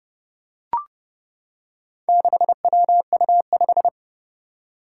Morse code, heard as a keyed tone, about 29 wpm, 700 Hz. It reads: 6WU5
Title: Four Challenging Letter-Number Combinations 30wpm